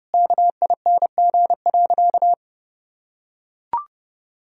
Morse code, heard as a keyed tone, about 30 wpm, 700 Hz. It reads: KING.
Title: Sentences from Top 1000 Words - Review Part 2 - 30wpm